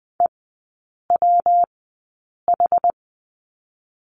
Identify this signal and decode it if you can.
Morse code, heard as a keyed tone, about 20 wpm, 700 Hz. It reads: E W H
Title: 212 Lesson 34e - Number 8 - Three Characters 20wpm